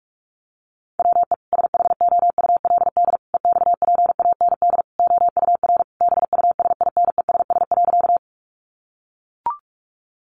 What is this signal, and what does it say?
Morse code, heard as a keyed tone, about 44 wpm, 700 Hz. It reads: WE SHOULD EXPAND OUR BUSINESS.
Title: Sentences from Top 1400 Words 45wpm